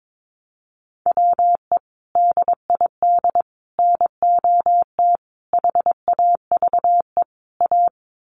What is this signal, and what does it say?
Morse code, heard as a keyed tone, about 22 wpm, 700 Hz. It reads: WE DID NOT HAVE A
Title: Sets of 5 Words Encore2 22wpm